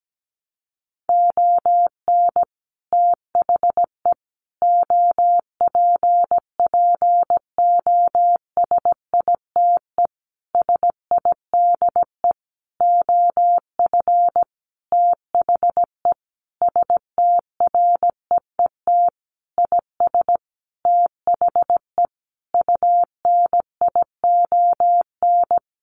Morse code, heard as a keyed tone, about 17 wpm, 700 Hz. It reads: ON THE OPPOSITE SIDE OF THE STREET IS THE UNION
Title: Sentences from Top 1400 Words 17wpm